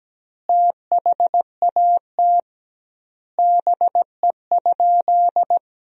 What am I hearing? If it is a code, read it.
Morse code, heard as a keyed tone, about 17 wpm, 700 Hz. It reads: THAT BE?
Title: 2x Word-Spacing - Sentences from Top 100 Words - 17wpm